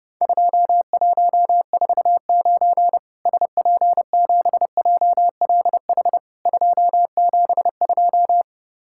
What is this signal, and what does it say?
Morse code, heard as a keyed tone, about 32 wpm, 700 Hz. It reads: 2149 SP7JLH 272
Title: Worked All Europe DX Contest - QTC Exchange - 30wpm